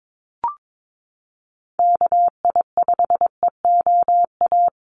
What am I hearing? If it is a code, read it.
Morse code, heard as a keyed tone, about 22 wpm, 700 Hz. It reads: KI5EOA
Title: USA Call Signs Repeated 3x 22wpm